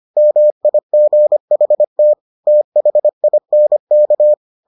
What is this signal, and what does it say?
Morse code, heard as a keyed tone, about 25 wpm, 600 Hz. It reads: MIGHT THINK